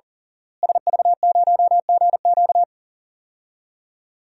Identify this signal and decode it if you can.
Morse code, heard as a keyed tone, about 40 wpm, 700 Hz. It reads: SV0GQ
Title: DXCC-100 Call Signs - 5 Characters - Repeated 3x - 40wpm